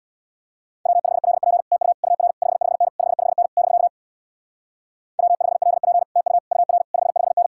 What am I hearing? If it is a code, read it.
Morse code, heard as a keyed tone, about 67 wpm, 700 Hz. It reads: W5LP TU UR 55N 55N <BK> W5LP TU UR 55N